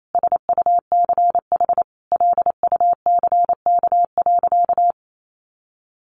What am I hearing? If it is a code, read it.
Morse code, heard as a keyed tone, about 28 wpm, 700 Hz. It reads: SUCH LUCK.
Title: Sentences from Top 1500 Words 28wpm